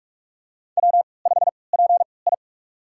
Morse code, heard as a keyed tone, about 46 wpm, spaced as for 26 wpm, 700 Hz. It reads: W5PI
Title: USA Call Signs - 4 Characters - 3x Character Spacing - 45wpm